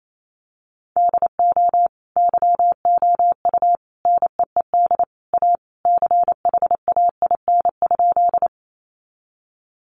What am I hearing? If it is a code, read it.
Morse code, heard as a keyed tone, about 28 wpm, 700 Hz. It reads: DO YOU NEED A CHAIN?
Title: Sentences from Top 1500 Words 28wpm